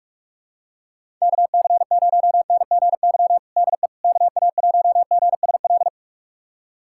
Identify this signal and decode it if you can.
Morse code, heard as a keyed tone, about 46 wpm, 700 Hz. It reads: KC0NGY DE KA1GSL
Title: USA Callsign DE Callsign - 6 Characters - 45wpm